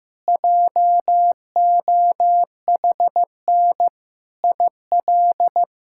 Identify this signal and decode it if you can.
Morse code, heard as a keyed tone, about 15 wpm, 700 Hz. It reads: JOHN IL
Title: K1USN SST QSO Practice - 15wpm